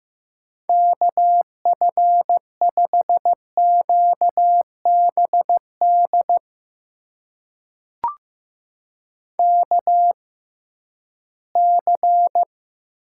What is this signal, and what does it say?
Morse code, heard as a keyed tone, about 15 wpm, spaced as for 11 wpm, 700 Hz. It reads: KF5QBD KC
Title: USA Call Signs - 6x Character Spacing - 15wpm